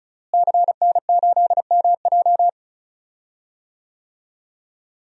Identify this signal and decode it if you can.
Morse code, heard as a keyed tone, about 36 wpm, 700 Hz. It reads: CN8MJ
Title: DXCC-100 Call Signs - 5 Characters - Repeated 2x - 35wpm